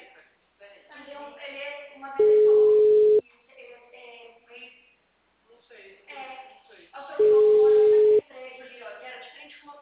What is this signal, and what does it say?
Ringback tone (Europe, 425 Hz), with speech.